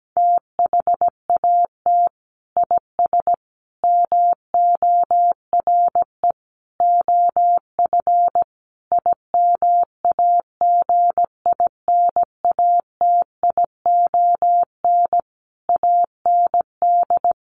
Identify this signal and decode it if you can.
Morse code, heard as a keyed tone, about 17 wpm, 700 Hz. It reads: THAT IS MORE OF IMAGINATION AND
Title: Sentences from Top 1300 Words 17wpm